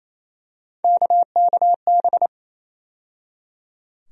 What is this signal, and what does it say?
Morse code, heard as a keyed tone, about 28 wpm, 700 Hz. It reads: KKB
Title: USA Call Sign Postfixes - 3 Letters 28wpm